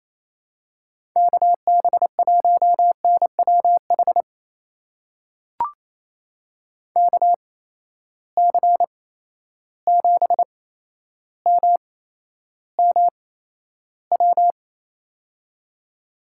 Morse code, heard as a keyed tone, about 28 wpm, spaced as for 12 wpm, 700 Hz. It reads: KB1NWH KC7MMW
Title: USA Call Signs - 6 Characters - 8x Character Spacing - 28wpm